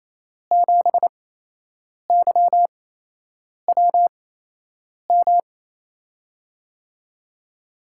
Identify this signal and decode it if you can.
Morse code, heard as a keyed tone, about 28 wpm, spaced as for 8 wpm, 700 Hz. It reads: 7YWM
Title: USA Call Signs - 5 Characters - 8x Character Spacing - 28wpm